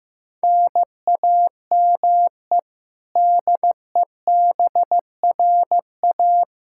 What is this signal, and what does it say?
Morse code, heard as a keyed tone, about 15 wpm, 700 Hz. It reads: NAME DEBRA
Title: Practice QSO - Exchange 1 - Beginning - 15wpm